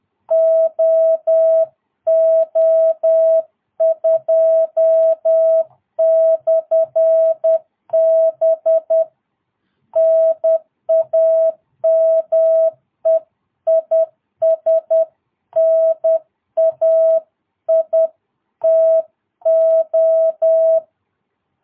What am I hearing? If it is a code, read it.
Morse code, heard as a keyed tone, about 10 wpm, 660 Hz. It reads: OO2/B NAME IS NAITO
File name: jh8cbh / 2022cw